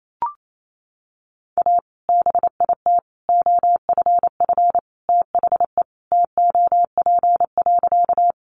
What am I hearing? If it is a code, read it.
Morse code, heard as a keyed tone, about 28 wpm, 700 Hz. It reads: A BIT OFF THE TOP.
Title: Sentences from Top 900 Words 28wpm